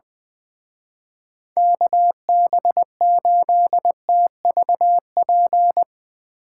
Morse code, heard as a keyed tone, about 21 wpm, 700 Hz. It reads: KB8TVP
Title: Speed Racing - USA Call Signs 20wpm